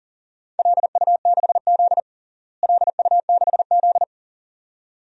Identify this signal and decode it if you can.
Morse code, heard as a keyed tone, about 40 wpm, 700 Hz. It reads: LU6Z LU6Z
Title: DXCC-100 Call Signs - 4 Characters - Repeated 2x - 40wpm